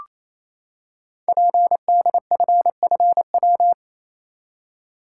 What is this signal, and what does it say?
Morse code, heard as a keyed tone, about 29 wpm, 700 Hz. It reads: PDFFW
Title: Code Groups - 5 Letter:Numbers - 28wpm